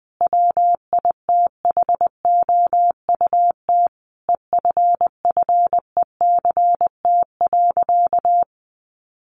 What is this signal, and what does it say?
Morse code, heard as a keyed tone, about 20 wpm, 700 Hz. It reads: WITHOUT EFFECT.
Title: Sentences from Top 1400 Words 20wpm